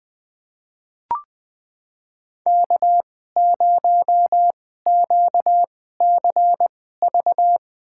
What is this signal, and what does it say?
Morse code, heard as a keyed tone, about 20 wpm, spaced as for 14 wpm, 700 Hz. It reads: K0QCV
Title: USA Call Signs - 2x Character Spacing - 20wpm